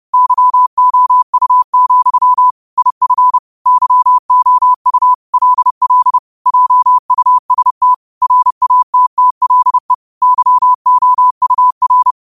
Morse code, heard as a keyed tone, about 30 wpm, 1000 Hz. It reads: YOU, IF YOU LL JUST RATTLE YOUR